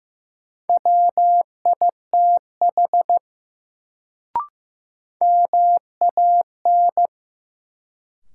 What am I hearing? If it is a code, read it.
Morse code, heard as a keyed tone, about 15 wpm, 700 Hz. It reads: WITH MAN
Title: Sets of 2 Words - New Word Drill - 15wpm